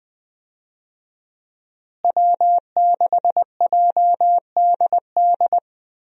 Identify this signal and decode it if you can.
Morse code, heard as a keyed tone, about 20 wpm, 700 Hz. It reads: W6JDD
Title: USA Call Signs - 5 Characters - Repeated 2x - 20wpm